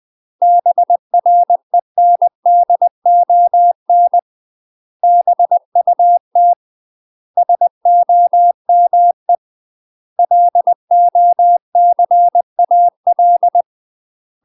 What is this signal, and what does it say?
Morse code, heard as a keyed tone, about 20 wpm, 700 Hz. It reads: BRENDON BUT SOME LOCAL
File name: UK MASTER QSOs TRAINING / 20WPM / UK MASTER QSOs 0005